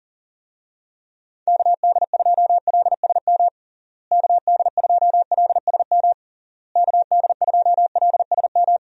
Morse code, heard as a keyed tone, about 40 wpm, 700 Hz. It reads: KD2LSM KD2LSM KD2LSM
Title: USA Call Signs - 6 Characters - Repeated 3x - 40wpm